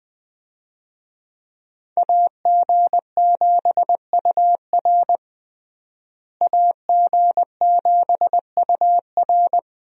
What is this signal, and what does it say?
Morse code, heard as a keyed tone, about 20 wpm, 700 Hz. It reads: AG7UR AG7UR
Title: USA Call Signs - 5 Characters - Repeated 2x - 20wpm